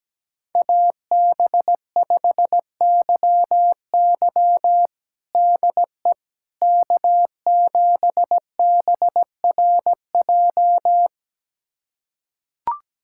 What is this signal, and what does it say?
Morse code, heard as a keyed tone, about 17 wpm, 700 Hz. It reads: AB5YY DE K7BRJ
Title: USA Callsign DE Callsign - 5 Characters - 17wpm